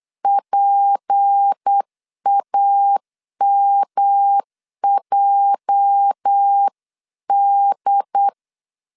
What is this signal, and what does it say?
Morse code, heard as a keyed tone, about 8 wpm, 800 Hz. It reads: PAMJD
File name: LEZIONI / 10 SIGNS / 10B INT 8wpm 3534